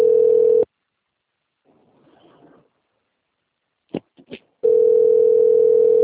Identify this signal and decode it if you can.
Ringback tone (North America), 440 + 480 Hz.